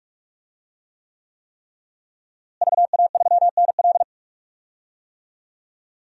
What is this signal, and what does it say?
Morse code, heard as a keyed tone, about 45 wpm, 700 Hz. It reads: VA3NL
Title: DXCC-100 Call Signs - 5 Characters - Repeated 3x - 45wpm